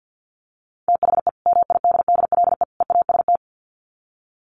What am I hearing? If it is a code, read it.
Morse code, heard as a keyed tone, about 50 wpm, 700 Hz. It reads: THE MIDDLE EAST
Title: Sets of 3 Words 50wpm